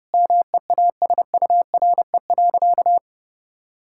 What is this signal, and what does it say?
Morse code, heard as a keyed tone, about 30 wpm, 700 Hz. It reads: MEASURE.
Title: Sentences from Top 1000 Words - Review Part 6 - 30wpm